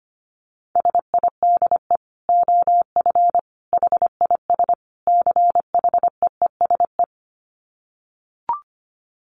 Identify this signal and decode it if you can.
Morse code, heard as a keyed tone, about 25 wpm, 700 Hz. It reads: SIDE OF HIS CHEESE
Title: Dit Challenge 25wpm